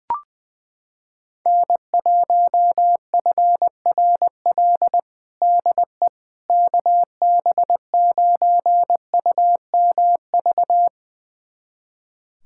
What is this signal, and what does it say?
Morse code, heard as a keyed tone, about 21 wpm, 700 Hz. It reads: N1FRL DE KB9UMV
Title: USA Callsign DE Callsign 20wpm